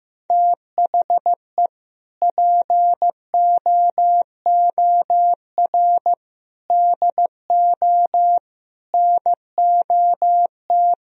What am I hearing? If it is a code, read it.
Morse code, heard as a keyed tone, about 15 wpm, 700 Hz. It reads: THE POOR DO NOT
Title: Sentences from Top 1000 Words - Review Part 2 - 15wpm